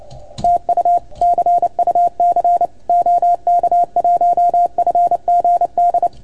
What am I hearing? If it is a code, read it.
Morse code, heard as a keyed tone, about 28 wpm, 690 Hz. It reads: TU CUC OK1FGD
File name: Archiv / CUC310 / OK1FGD CUC 310 pileup